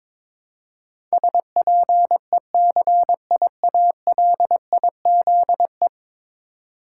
Morse code, heard as a keyed tone, about 23 wpm, 700 Hz. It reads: SPECIALIZE
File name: practice / 22wpm / 10 Letter Words - First 3 Letters Given - 22wpm